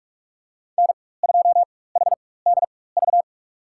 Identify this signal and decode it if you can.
Morse code, heard as a keyed tone, about 48 wpm, spaced as for 21 wpm, 700 Hz. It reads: N2HDV
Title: USA Call Signs - 5 Characters - 4x Character Spacing - 45wpm